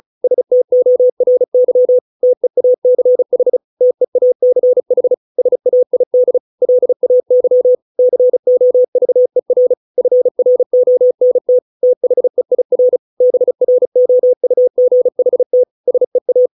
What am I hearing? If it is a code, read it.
Morse code, heard as a keyed tone, about 34 wpm, 500 Hz. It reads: STORY TEACH TEACH SAID LAY COVER FRONT THEIR BROUGHT SEA